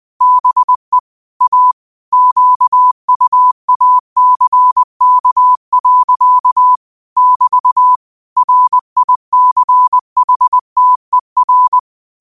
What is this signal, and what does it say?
Morse code, heard as a keyed tone, about 20 wpm, 1000 Hz. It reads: BE A QUACK. <BT> RICHTER
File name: qotdfiles / 20wpm